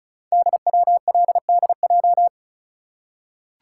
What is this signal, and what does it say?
Morse code, heard as a keyed tone, about 36 wpm, 700 Hz. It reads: DWLDJ
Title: Code Groups - 5 Letters - 35wpm